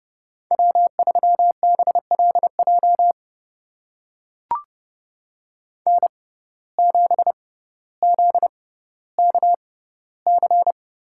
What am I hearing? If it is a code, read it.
Morse code, heard as a keyed tone, about 31 wpm, spaced as for 15 wpm, 700 Hz. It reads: W3BLJ N7ZKC
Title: USA Call Signs - 6x Character Spacing - 30wpm